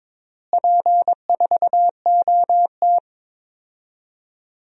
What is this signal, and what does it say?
Morse code, heard as a keyed tone, about 22 wpm, 700 Hz. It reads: P4OT
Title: DXCC-100 Call Signs - 4 Characters - Repeated 2x - 22wpm